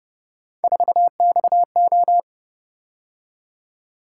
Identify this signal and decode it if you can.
Morse code, heard as a keyed tone, about 31 wpm, 700 Hz. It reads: 4XO